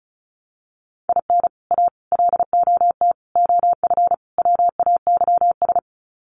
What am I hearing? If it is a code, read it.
Morse code, heard as a keyed tone, about 34 wpm, 700 Hz. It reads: IN A LOT OF WAYS
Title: Sets of 5 Words Encore 35wpm